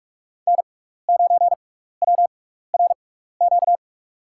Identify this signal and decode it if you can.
Morse code, heard as a keyed tone, about 48 wpm, spaced as for 16 wpm, 690 Hz. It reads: N9WRQ
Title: USA Call Signs - 5 Characters - 6x Character Spacing - 45wpm